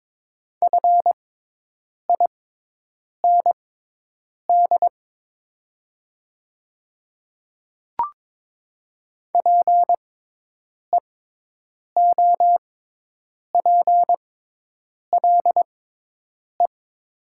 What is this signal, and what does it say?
Morse code, heard as a keyed tone, about 22 wpm, spaced as for 8 wpm, 700 Hz. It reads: FIND PEOPLE
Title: 6x Character Spacing - Top 100 Words - 22wpm